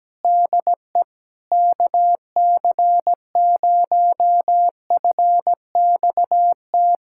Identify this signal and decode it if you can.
Morse code, heard as a keyed tone, about 17 wpm, 700 Hz. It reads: DE KC0FXT
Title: USA Callsign DE Callsign - 6 Characters - 17wpm